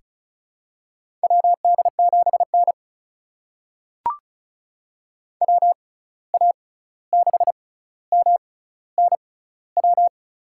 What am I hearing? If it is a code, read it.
Morse code, heard as a keyed tone, about 36 wpm, spaced as for 16 wpm, 700 Hz. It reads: WD7N WA6MNW
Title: USA Call Signs - 6x Character Spacing - 35wpm